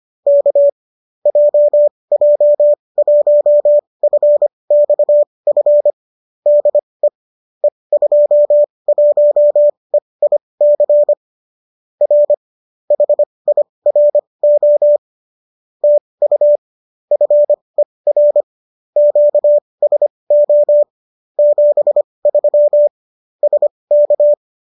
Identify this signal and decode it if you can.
Morse code, heard as a keyed tone, about 25 wpm, spaced as for 20 wpm, 590 Hz. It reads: K JJ1FXF DE E21EIC R HIRO TU FER QSO 73 SK
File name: qso208 20